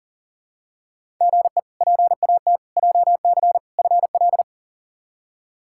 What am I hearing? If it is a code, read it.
Morse code, heard as a keyed tone, about 37 wpm, 700 Hz. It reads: GE PAT JC FL
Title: K1USN SST QSO Practice - 40wpm